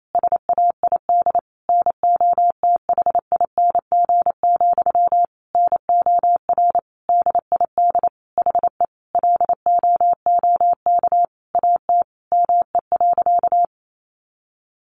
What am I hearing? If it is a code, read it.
Morse code, heard as a keyed tone, about 28 wpm, 700 Hz. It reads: SAID NOTHING, NOR DID HE LOOK AT ME.
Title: Sentences from Top 1500 Words 28wpm